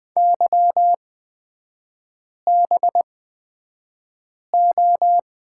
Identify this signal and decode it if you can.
Morse code, heard as a keyed tone, about 20 wpm, spaced as for 5 wpm, 700 Hz. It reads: YBO
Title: 243 Lesson 39e - Letter z - Three Characters 20wpm